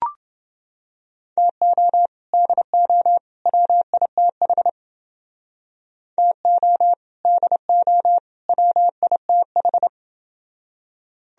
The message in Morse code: TO DO WITH TO DO WITH

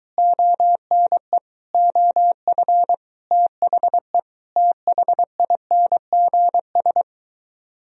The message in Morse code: ONE OF THE THINGS

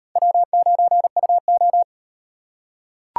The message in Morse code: W9UO

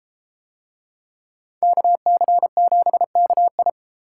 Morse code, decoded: KC7KI

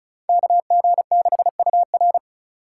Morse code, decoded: KG6UR

